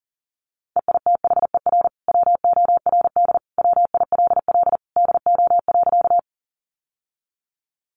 Morse code, heard as a keyed tone, about 40 wpm, 700 Hz. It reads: EITHER WORD WILL DO.
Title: Sentences from Top 600 Words 40wpm